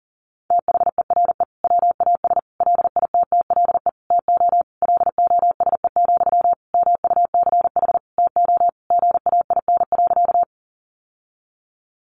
Morse code, heard as a keyed tone, about 40 wpm, 700 Hz. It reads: THERE WAS LITTLE TO LOSE, MUCH TO GAIN.
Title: Sentences from Top 1100 Words 40wpm